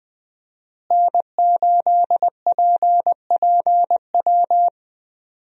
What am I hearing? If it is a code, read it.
Morse code, heard as a keyed tone, about 20 wpm, 700 Hz. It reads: N8PPW
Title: USA Call Signs - 5 Characters - Repeated 3x - 20wpm